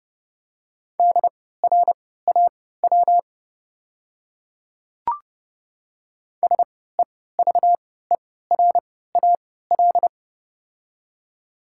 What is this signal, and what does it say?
Morse code, heard as a keyed tone, about 30 wpm, spaced as for 17 wpm, 700 Hz. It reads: DRAW SEVERAL
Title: 3x Character Spacing - Top 500 Words - 30wpm